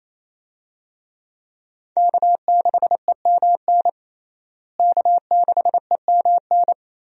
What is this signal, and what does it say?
Morse code, heard as a keyed tone, about 27 wpm, 700 Hz. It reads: K6EMN K6EMN
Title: USA Call Signs - 5 Characters - Repeated 2x - 28wpm